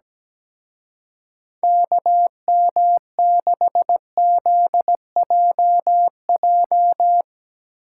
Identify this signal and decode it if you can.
Morse code, heard as a keyed tone, about 17 wpm, 700 Hz. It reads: KM6ZJJ
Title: Speed Racing - USA 2x3 Call Signs 17wpm